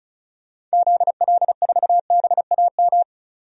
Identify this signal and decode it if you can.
Morse code, heard as a keyed tone, about 36 wpm, 690 Hz. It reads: ZL4BAM
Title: DXCC-100 Call Signs - 6 Characters - 35wpm